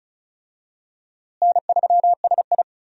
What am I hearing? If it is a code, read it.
Morse code, heard as a keyed tone, about 36 wpm, 700 Hz. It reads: N3SI
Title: USA Call Signs - 4 Characters - Repeated 2x - 35wpm